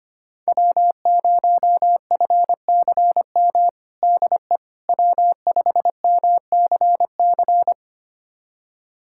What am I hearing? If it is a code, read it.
Morse code, heard as a keyed tone, about 25 wpm, 700 Hz. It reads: W0FCM DE W5MCC
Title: USA Callsign DE Callsign - 5 Characters - 25wpm